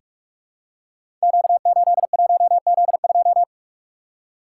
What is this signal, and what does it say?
Morse code, heard as a keyed tone, about 44 wpm, 690 Hz. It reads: Q81Z2
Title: Code Groups - 5 Letter:Numbers - 45wpm